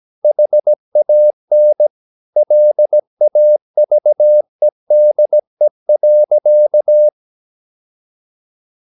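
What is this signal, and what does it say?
Morse code, heard as a keyed tone, about 17 wpm, 600 Hz. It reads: HAN LAVEDE.
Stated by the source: Slow (QRS) Morse practice at 17 wpm, from CW med Gnister.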